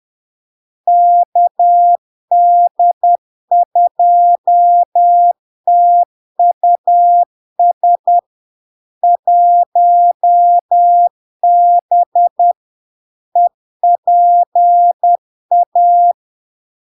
Morse code, heard as a keyed tone, about 10 wpm, 700 Hz. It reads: KD2TUS 1B EPA
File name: LARK-FD-2022-Log-in-CW-10wpm-700-Hz